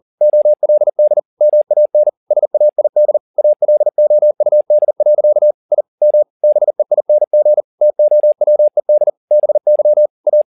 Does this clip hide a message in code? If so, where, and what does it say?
Yes: Morse code from 0.2 to 10.4 s, heard as a keyed tone, about 40 wpm, 600 Hz. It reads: OLD MAN SAID ALOUD. I M BEING TOWED BY A